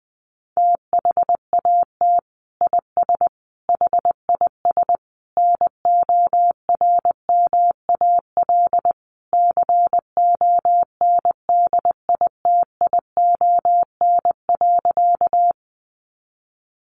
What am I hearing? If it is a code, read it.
Morse code, heard as a keyed tone, about 20 wpm, 700 Hz. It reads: THAT IS HIS NORMAL CONDITION.
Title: Sentences from Top 1200 Words 20wpm